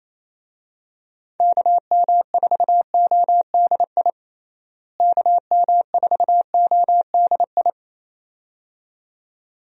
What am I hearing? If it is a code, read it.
Morse code, heard as a keyed tone, about 28 wpm, 700 Hz. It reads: KM4ODI KM4ODI
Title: USA Call Signs - 6 Characters - Repeated 2x - 28wpm